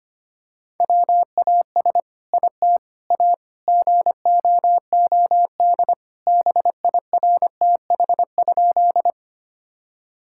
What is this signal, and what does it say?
Morse code, heard as a keyed tone, about 25 wpm, 700 Hz. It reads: WAS IT A GOOD BIRTH?
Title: Sentences from Top 1000 Words - Review Part 5 - 25wpm